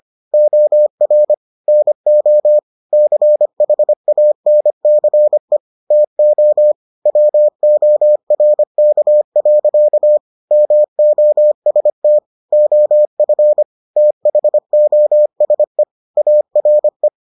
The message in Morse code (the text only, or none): OR NO CHANCE TO WORK. MOST OF THOSE ARE